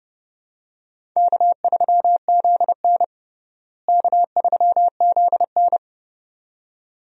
Morse code, heard as a keyed tone, about 30 wpm, 700 Hz. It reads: K3ZN K3ZN
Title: USA Call Signs - 4 Characters - Repeated 2x - 30wpm